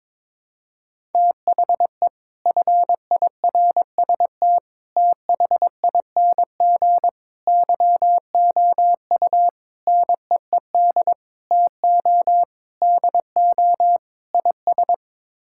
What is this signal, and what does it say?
Morse code, heard as a keyed tone, about 22 wpm, 700 Hz. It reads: THE FIRST THING YOU NEED TO DO IS
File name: practice / 22wpm / Sets of 8 Words - 22wpm